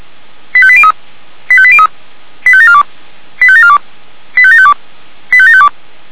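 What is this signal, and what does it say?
Five-tone selective call (ZVEI), heard as quick runs of five tones, tones of about 70 ms. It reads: 85803 85803 85742 85742 85752 85752
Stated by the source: ZVEI five-tone selective call received on an SDR.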